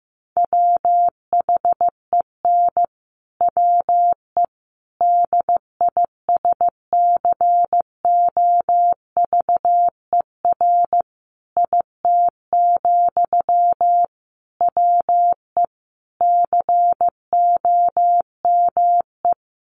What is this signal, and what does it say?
Morse code, heard as a keyed tone, about 15 wpm, 700 Hz. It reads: WHEN WE DISCOVER IT, WE COME
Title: Sentences from Top 1100 Words 15wpm